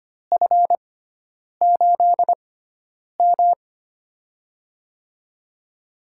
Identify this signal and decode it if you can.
Morse code, heard as a keyed tone, about 25 wpm, spaced as for 9 wpm, 700 Hz. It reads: F8M
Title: USA Call Signs - 4 Characters - 6x Character Spacing - 25wpm